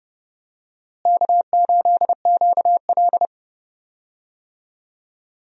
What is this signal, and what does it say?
Morse code, heard as a keyed tone, about 30 wpm, 700 Hz. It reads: K8QL
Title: USA Call Signs - 4 Characters - Repeated 3x - 30wpm